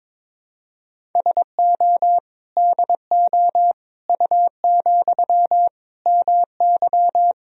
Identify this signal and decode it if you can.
Morse code, heard as a keyed tone, about 22 wpm, 700 Hz. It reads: SO DO U, MY
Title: Sentences from Top 400 Words and Abbreviations 22wpm